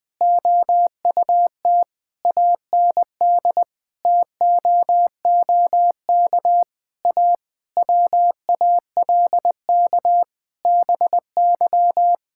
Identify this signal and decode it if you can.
Morse code, heard as a keyed tone, about 21 wpm, 700 Hz. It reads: OUT AND TOOK A WALK BY
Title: Sentences from Top 1600 Words 20wpm